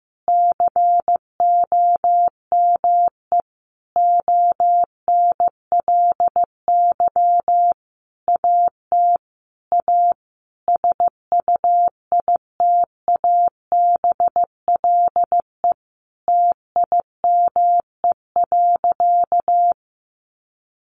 Morse code, heard as a keyed tone, about 15 wpm, 700 Hz. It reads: COME ONLY AT A SUITABLE TIME.
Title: Sentences from Top 1400 Words 15wpm